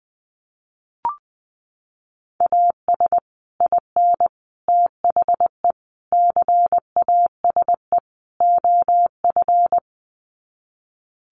Morse code, heard as a keyed tone, about 20 wpm, 700 Hz. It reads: AS IN THE CASE OF
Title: Sets of 5 Words 20wpm